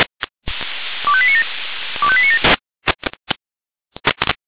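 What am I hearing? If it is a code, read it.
Five-tone selective call (ZVEI), heard as quick runs of five tones, tones of about 80 ms. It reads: 25897 25897
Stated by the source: ZVEI five-tone selective call received on an SDR.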